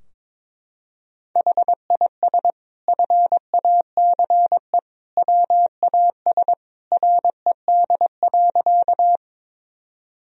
Morse code, heard as a keyed tone, about 22 wpm, 700 Hz. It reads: HIS FACE WAS RED.
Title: Sentences from Top 1000 Words - Review Part 2 - 22wpm